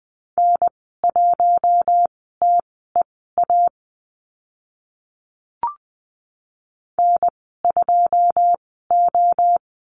Morse code, heard as a keyed tone, about 19 wpm, spaced as for 14 wpm, 700 Hz. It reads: N1TEA N2O
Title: 082 Lesson 14g - Number 2 - Call Signs 20wpm